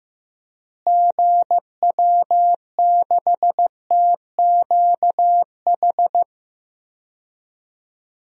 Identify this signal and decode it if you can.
Morse code, heard as a keyed tone, about 15 wpm, 700 Hz. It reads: GW6TQH
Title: DXCC-100 Call Signs - 6 Characters - Repeated 2x - 15wpm